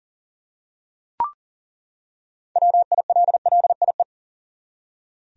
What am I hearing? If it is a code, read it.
Morse code, heard as a keyed tone, about 40 wpm, 700 Hz. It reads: WILLIE
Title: USA Common Names - 40wpm